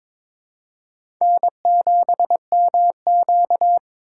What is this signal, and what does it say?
Morse code, heard as a keyed tone, about 22 wpm, 700 Hz. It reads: N7MQ
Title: USA Call Signs - 4 Characters - 4x Character Spacing - 22wpm